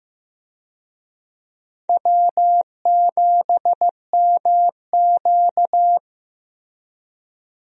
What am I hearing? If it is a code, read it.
Morse code, heard as a keyed tone, about 14 wpm, 700 Hz. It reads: W7MQ
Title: USA Call Signs - 4 Characters - Repeated 3x - 15wpm